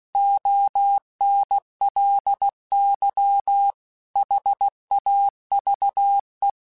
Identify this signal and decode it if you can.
Morse code, heard as a keyed tone, about 16 wpm, 800 Hz. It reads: ONLY HAVE
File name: Technical Subjects / Technical Subjects-mp3s / AND Gate @15